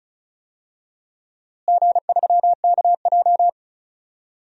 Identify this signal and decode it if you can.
Morse code, heard as a keyed tone, about 34 wpm, 700 Hz. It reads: G3KJ
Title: DXCC-100 Call Signs - 4 Characters - Repeated 2x - 35wpm